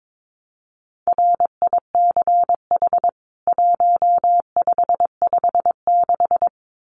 Morse code, heard as a keyed tone, about 22 wpm, 700 Hz. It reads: RICH 1556
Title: CWT Exchange 22wpm